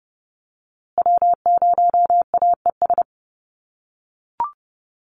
Morse code, heard as a keyed tone, about 30 wpm, 700 Hz. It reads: W0AES
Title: USA Call Signs Repeated 3x 30wpm